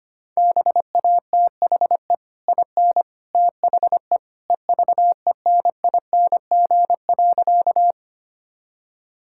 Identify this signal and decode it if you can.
Morse code, heard as a keyed tone, about 25 wpm, 690 Hz. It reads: BATHE IN THE EVENING.
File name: Easy Sentences with Hints - 25wpm